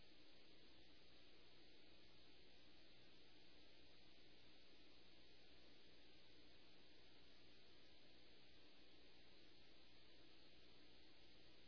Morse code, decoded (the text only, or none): none